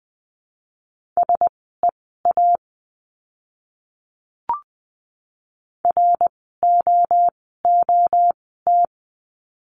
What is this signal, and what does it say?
Morse code, heard as a keyed tone, about 20 wpm, spaced as for 15 wpm, 700 Hz. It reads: SEA ROOT
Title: 061 Lesson 11f - Letter h - Words 20wpm